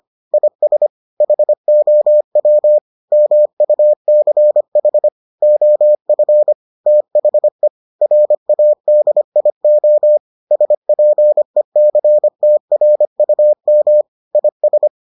Morse code, heard as a keyed tone, about 25 wpm, 600 Hz. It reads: IS HOW MUCH OF THE RADIO SPECTRUM IS